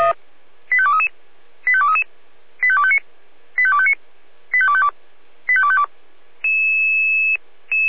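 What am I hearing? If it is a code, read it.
Five-tone selective call (ZVEI), heard as quick runs of five tones, tones of about 70 ms. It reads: 85233 85233 85259 85259 85252 85252
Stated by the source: ZVEI five-tone selective call received with an SDR.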